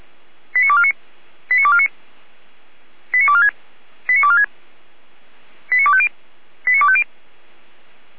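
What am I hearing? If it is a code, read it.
Five-tone selective call (ZVEI), heard as quick runs of five tones, tones of about 70 ms. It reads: 89249 89249 89246 89246 89250 89250
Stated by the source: ZVEI five-tone selective call 89166, received by SDR on the 455 channel.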